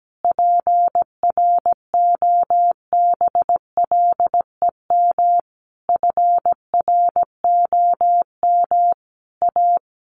Morse code, heard as a keyed tone, about 17 wpm, 700 Hz. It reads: PROBLEM FROM A
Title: Sentences from Top 1500 Words 17wpm